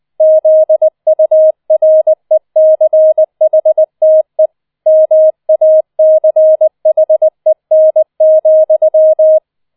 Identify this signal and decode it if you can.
Morse code, heard as a keyed tone, about 20 wpm, 630 Hz. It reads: ZURECHTE MACHEN,